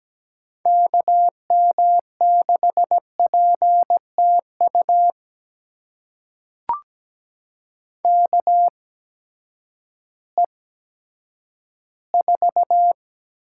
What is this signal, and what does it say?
Morse code, heard as a keyed tone, about 17 wpm, spaced as for 10 wpm, 700 Hz. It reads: KM6PTU KE4
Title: USA Call Signs - 6 Characters - 8x Character Spacing - 17wpm